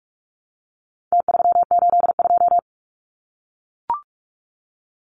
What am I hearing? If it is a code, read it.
Morse code, heard as a keyed tone, about 44 wpm, 700 Hz. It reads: T382
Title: Numbers - Contest Style - 4-Digit 45wpm